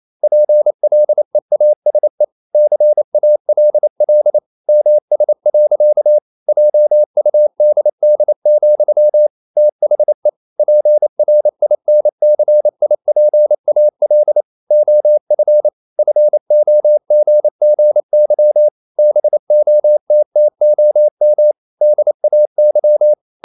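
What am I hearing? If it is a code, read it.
Morse code, heard as a keyed tone, about 28 wpm, 600 Hz. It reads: PLEASE CALL MS. JUDD, THE PRINCIPAL OF FOGGY BOTTOM DAY
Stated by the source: Morse code practice at 28 wpm, text of Ken Follett's novel Never in English, from CW med Gnister.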